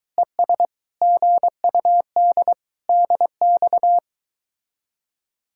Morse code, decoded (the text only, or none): ES GUD DX